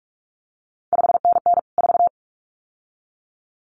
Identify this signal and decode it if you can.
Morse code, heard as a keyed tone, about 46 wpm, 700 Hz. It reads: 5NN 4
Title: CQ WW DX Contest Exchange - 45wpm